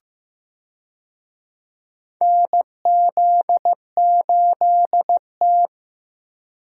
Morse code, heard as a keyed tone, about 15 wpm, 700 Hz. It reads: NZ8T
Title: USA Call Signs - 4 Characters - Repeated 3x - 15wpm